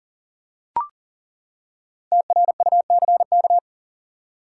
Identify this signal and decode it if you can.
Morse code, heard as a keyed tone, about 40 wpm, 700 Hz. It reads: TRUCK
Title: Five Letter Words - 40wpm